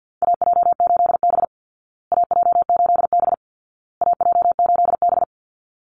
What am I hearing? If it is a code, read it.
Morse code, heard as a keyed tone, about 50 wpm, 700 Hz. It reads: AJ8B AJ8B AJ8B
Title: Call Signs Repeated 2x - 50wpm